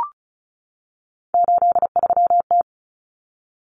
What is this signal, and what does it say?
Morse code, heard as a keyed tone, about 34 wpm, 700 Hz. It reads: 83T
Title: Common Cut Numbers - 3-Digit 35wpm